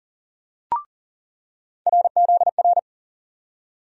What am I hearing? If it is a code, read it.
Morse code, heard as a keyed tone, about 40 wpm, 700 Hz. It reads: RZR
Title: USA Call Sign Postfixes - 3 Letters 40wpm